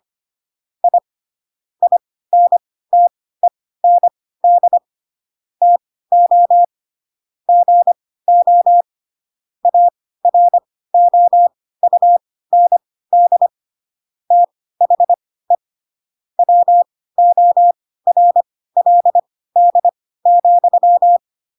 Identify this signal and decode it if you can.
Morse code, heard as a keyed tone, about 25 wpm, spaced as for 16 wpm, 700 Hz. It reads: I INTEND TO GO AROUND THE WORLD,
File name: Pinocchio / Pinocchio ch2 10 25 ik1hge